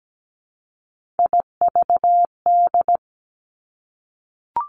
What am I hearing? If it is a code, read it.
Morse code, heard as a keyed tone, about 17 wpm, 700 Hz. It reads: IVD
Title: Three Challenging Letter-Number Combinations 17wpm